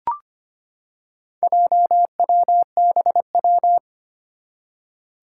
Morse code, heard as a keyed tone, about 25 wpm, 700 Hz. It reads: JWBW
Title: Four Challenging Letter-Number Combinations 25wpm